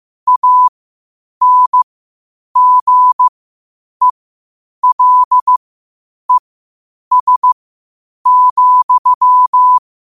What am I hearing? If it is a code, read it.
Morse code, heard as a keyed tone, about 15 wpm, spaced as for 8 wpm, 1000 Hz. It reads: ANGELES,